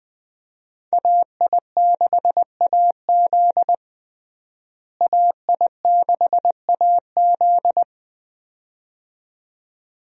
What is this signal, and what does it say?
Morse code, heard as a keyed tone, about 20 wpm, 700 Hz. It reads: AI6AZ AI6AZ
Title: USA Call Signs - 5 Characters - Repeated 2x - 20wpm